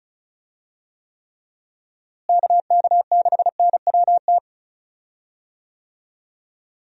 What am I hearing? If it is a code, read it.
Morse code, heard as a keyed tone, about 34 wpm, 700 Hz. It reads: KK6NWT
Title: USA Call Signs - 6 Characters - Repeated 2x - 35wpm